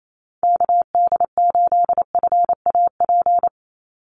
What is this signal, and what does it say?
Morse code, heard as a keyed tone, about 28 wpm, 700 Hz. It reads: KD8FAP
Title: USA Call Signs 28wpm